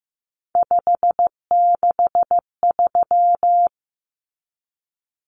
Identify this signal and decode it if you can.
Morse code, heard as a keyed tone, about 15 wpm, 700 Hz. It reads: 563
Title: Common Cut Numbers - 3-Digit 15wpm